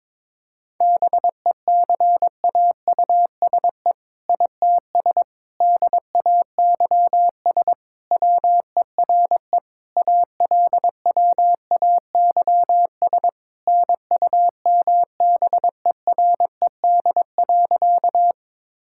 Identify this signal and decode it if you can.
Morse code, heard as a keyed tone, about 22 wpm, 690 Hz. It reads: BECAUSE ITS DAYS WERE ALWAYS NUMBERED.
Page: morse-code-ninja-practice-22wpm